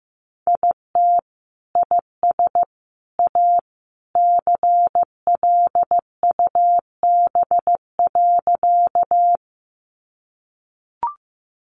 Morse code, heard as a keyed tone, about 15 wpm, 700 Hz. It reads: IT IS A CLUB.
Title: Sentences from Top 1200 Words 15wpm